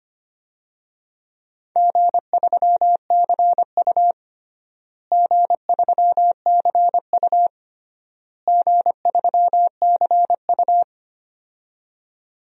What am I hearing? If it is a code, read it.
Morse code, heard as a keyed tone, about 25 wpm, 700 Hz. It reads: G3CU G3CU G3CU
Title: DXCC-100 Call Signs - 4 Characters - Repeated 3x - 25wpm